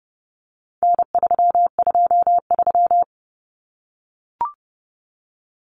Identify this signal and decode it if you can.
Morse code, heard as a keyed tone, about 30 wpm, 700 Hz. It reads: N323
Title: Common Cut Numbers - 4-Digit 30wpm